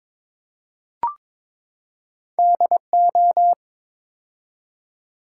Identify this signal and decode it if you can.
Morse code, heard as a keyed tone, about 22 wpm, 700 Hz. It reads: DO
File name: practice / 22wpm / Build A Phrase - 5-Word Phrases - 22wpm